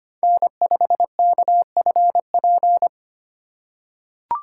Morse code, heard as a keyed tone, about 25 wpm, 700 Hz. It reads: N5KFP
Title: USA Call Signs - 2x Character Spacing - 25wpm